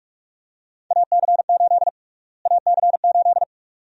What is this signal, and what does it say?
Morse code, heard as a keyed tone, about 46 wpm, 700 Hz. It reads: AC8 AC8
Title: USA Call Sign Prefixes - Repeated 2x - 45wpm